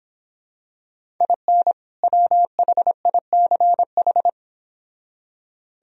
Morse code, heard as a keyed tone, about 26 wpm, 700 Hz. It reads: IN WHICH